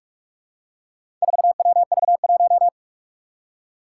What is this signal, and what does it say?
Morse code, heard as a keyed tone, about 44 wpm, 700 Hz. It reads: 4WV1
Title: Four Challenging Letter-Number Combinations 45wpm